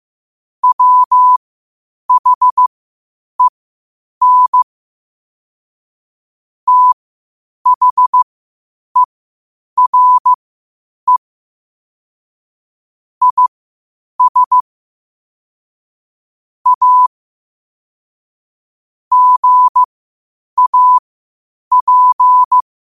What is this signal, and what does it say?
Morse code, heard as a keyed tone, about 15 wpm, spaced as for 8 wpm, 1000 Hz. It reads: WHEN THERE IS A GAP
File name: qotdfiles / 8wpm